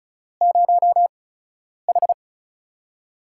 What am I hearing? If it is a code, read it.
Morse code, heard as a keyed tone, about 36 wpm, spaced as for 10 wpm, 700 Hz. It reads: 0H
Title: USA Call Signs - 4 Characters - 8x Character Spacing - 35wpm